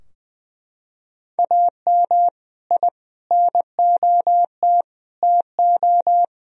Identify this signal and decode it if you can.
Morse code, heard as a keyed tone, about 21 wpm, 700 Hz. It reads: AM I NOT TO
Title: Sentences from Top 1000 Words - Review Part 2 - 20wpm